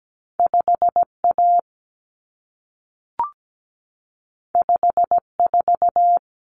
Morse code, heard as a keyed tone, about 17 wpm, 700 Hz. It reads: 5A 54
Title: Common Cut Numbers - 2-Digit 17wpm